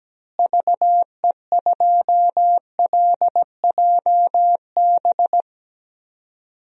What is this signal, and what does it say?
Morse code, heard as a keyed tone, about 17 wpm, 700 Hz. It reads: VE2LJB
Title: DXCC-100 Call Signs - 6 Characters - Repeated 2x - 17wpm